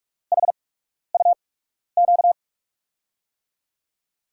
Morse code, heard as a keyed tone, about 47 wpm, spaced as for 12 wpm, 700 Hz. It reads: HUQ
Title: USA Call Signs - 6 Characters - 8x Character Spacing - 45wpm